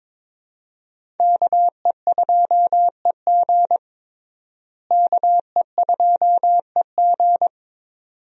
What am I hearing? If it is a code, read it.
Morse code, heard as a keyed tone, about 22 wpm, 700 Hz. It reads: KE2EG KE2EG
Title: USA Call Signs - 5 Characters - Repeated 2x - 22wpm